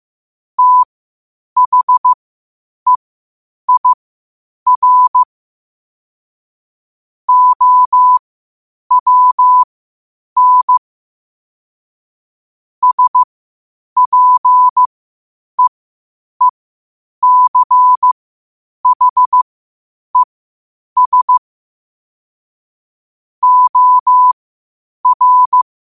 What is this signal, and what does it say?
Morse code, heard as a keyed tone, about 15 wpm, spaced as for 8 wpm, 1000 Hz. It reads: THEIR OWN SPEECHES OR
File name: qotdfiles / 8wpm